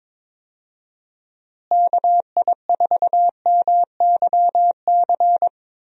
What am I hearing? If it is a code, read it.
Morse code, heard as a keyed tone, about 21 wpm, 700 Hz. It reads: KI4MYC
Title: USA Call Signs - 6 Characters - Repeated 3x - 22wpm